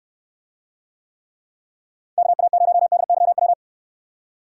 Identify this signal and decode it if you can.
Morse code, heard as a keyed tone, about 68 wpm, 700 Hz. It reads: KA0DYW